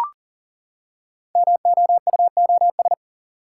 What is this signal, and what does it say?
Morse code, heard as a keyed tone, about 38 wpm, 700 Hz. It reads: MOUOS
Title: Code Groups - 5 Letters - 40wpm